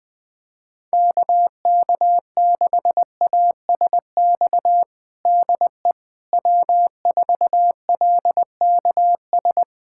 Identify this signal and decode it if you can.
Morse code, heard as a keyed tone, about 20 wpm, 700 Hz. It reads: KK6ASX DE W4LKS
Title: USA Callsign DE Callsign 20wpm